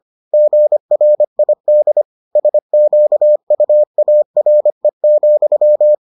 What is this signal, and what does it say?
Morse code, heard as a keyed tone, about 25 wpm, 600 Hz. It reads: GRID SQUARE,